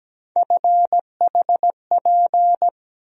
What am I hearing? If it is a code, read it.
Morse code, heard as a keyed tone, about 17 wpm, 700 Hz. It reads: FHP
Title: USA Call Sign Postfixes - 3 Letters 17wpm